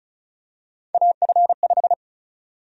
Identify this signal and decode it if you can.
Morse code, heard as a keyed tone, about 36 wpm, 700 Hz. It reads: AF5
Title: USA Call Sign Prefixes - 35wpm